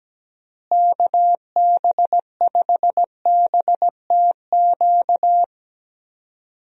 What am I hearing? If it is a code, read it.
Morse code, heard as a keyed tone, about 17 wpm, 700 Hz. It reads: KB5BTQ
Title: USA Call Signs - 6 Characters - Repeated 3x - 17wpm